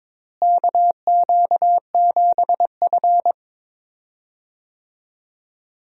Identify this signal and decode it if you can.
Morse code, heard as a keyed tone, about 22 wpm, 700 Hz. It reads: KQ7F
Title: USA Call Signs - 4 Characters - Repeated 2x - 22wpm